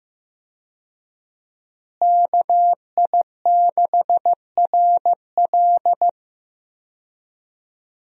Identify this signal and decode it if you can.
Morse code, heard as a keyed tone, about 15 wpm, 700 Hz. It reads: KI6RL
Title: USA Call Signs - 5 Characters - Repeated 2x - 15wpm